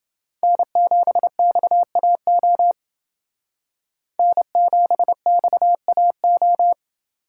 Morse code, heard as a keyed tone, about 29 wpm, 700 Hz. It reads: N7XAO N7XAO